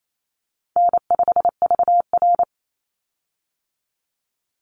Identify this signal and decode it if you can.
Morse code, heard as a keyed tone, about 28 wpm, 700 Hz. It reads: N5VR
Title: Easy Call Signs - 28wpm